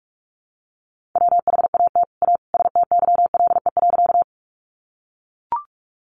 Morse code, heard as a keyed tone, about 44 wpm, 700 Hz. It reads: WHAT A STYLE.
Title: Sentences from Top 800 Words 45wpm